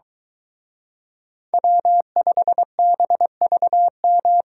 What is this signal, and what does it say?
Morse code, heard as a keyed tone, about 22 wpm, 700 Hz. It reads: W5BVM